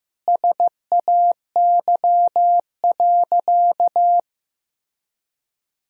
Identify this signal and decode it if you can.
Morse code, heard as a keyed tone, about 15 wpm, 700 Hz. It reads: SAY.
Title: Sentences from Top 1000 Words - Review Part 2 - 15wpm